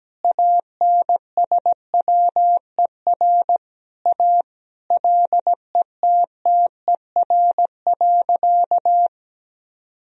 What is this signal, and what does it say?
Morse code, heard as a keyed tone, about 17 wpm, 700 Hz. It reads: ANSWER A LETTER.
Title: Sentences from Top 1000 Words - Review Part 1 - 17wpm